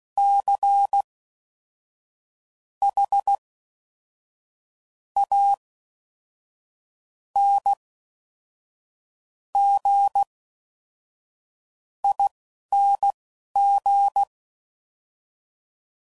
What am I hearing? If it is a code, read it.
Morse code, heard as a keyed tone, about 16 wpm, spaced as for 11 wpm, 800 Hz. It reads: C H A N G ING
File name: Suffixes / Special INGs / Fast INGs